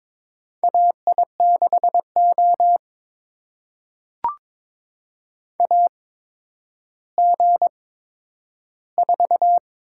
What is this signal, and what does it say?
Morse code, heard as a keyed tone, about 21 wpm, spaced as for 11 wpm, 700 Hz. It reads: AI6O AG4
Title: USA Call Signs - 8x Character Spacing - 22wpm